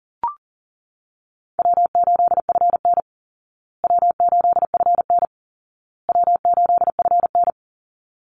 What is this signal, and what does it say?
Morse code, heard as a keyed tone, about 40 wpm, 700 Hz. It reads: W8FN W8FN W8FN
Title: Call Signs Repeated 2x - 40wpm